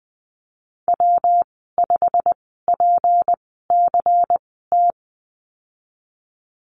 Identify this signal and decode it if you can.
Morse code, heard as a keyed tone, about 20 wpm, spaced as for 15 wpm, 700 Hz. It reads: W5PCT
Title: 173 Lesson 28g - Number 9 - Call Signs 20wpm